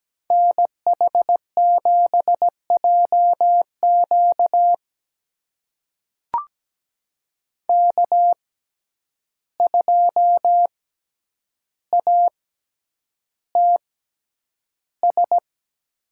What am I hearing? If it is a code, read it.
Morse code, heard as a keyed tone, about 17 wpm, spaced as for 9 wpm, 700 Hz. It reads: NH7JQ K2ATS
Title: USA Call Signs - 5 Characters - 6x Character Spacing - 17wpm